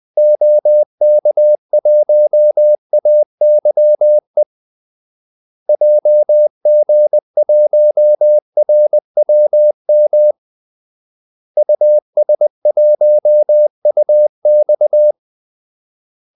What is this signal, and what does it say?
Morse code, heard as a keyed tone, about 20 wpm, 600 Hz. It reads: OK1AYE JG1RWM US1UX
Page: Kaldesignaler 20 wpm